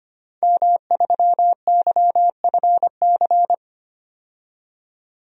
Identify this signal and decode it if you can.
Morse code, heard as a keyed tone, about 25 wpm, 700 Hz. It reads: M3YFC
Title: DXCC-100 Call Signs - 5 Characters - 25wpm